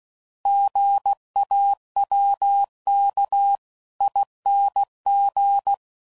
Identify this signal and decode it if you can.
Morse code, heard as a keyed tone, about 16 wpm, 800 Hz. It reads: GAWKING
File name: Prefixes / By First Letter / Gs